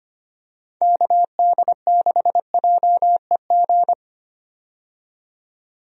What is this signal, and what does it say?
Morse code, heard as a keyed tone, about 25 wpm, 700 Hz. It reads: KD6JEG
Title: USA Call Signs - 6 Characters - 25wpm